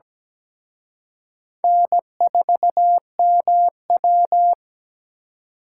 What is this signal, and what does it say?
Morse code, heard as a keyed tone, about 20 wpm, 700 Hz. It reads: N4MW
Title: Speed Racing - USA 1x2 Call Signs 17wpm